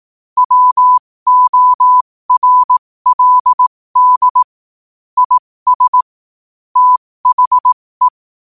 Morse code, heard as a keyed tone, about 18 wpm, spaced as for 15 wpm, 1000 Hz. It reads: WORLD IS THE
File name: qotdfiles / 15wpm